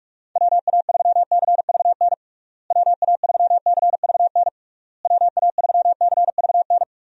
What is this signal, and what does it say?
Morse code, heard as a keyed tone, about 41 wpm, 700 Hz. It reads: WA3CVN WA3CVN WA3CVN
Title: USA Call Signs - 6 Characters - Repeated 3x - 45wpm